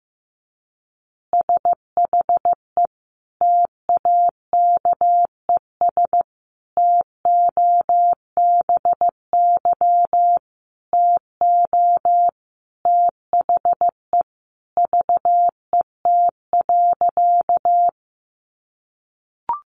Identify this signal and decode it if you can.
Morse code, heard as a keyed tone, about 15 wpm, 700 Hz. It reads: SHE TAKES TOBY TO THE VET.
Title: Easy Phrases 15wpm